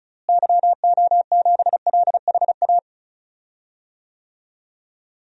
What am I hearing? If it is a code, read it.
Morse code, heard as a keyed tone, about 34 wpm, 700 Hz. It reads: YO7LHA
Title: DXCC-100 Call Signs - 6 Characters - Repeated 3x - 35wpm